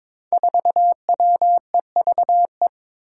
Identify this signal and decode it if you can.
Morse code, heard as a keyed tone, about 22 wpm, 700 Hz. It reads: 4WEVE
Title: Five Challenging Letter-Number Combinations 22wpm